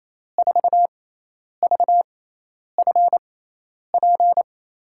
Morse code, heard as a keyed tone, about 28 wpm, spaced as for 10 wpm, 700 Hz. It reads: 4VFP
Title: USA Call Signs - 6 Characters - 6x Character Spacing - 28wpm